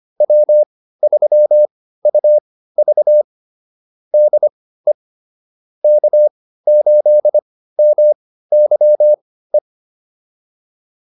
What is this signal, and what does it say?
Morse code, heard as a keyed tone, about 25 wpm, spaced as for 15 wpm, 600 Hz. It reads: W3UV DE K8MYE
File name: QSO403 15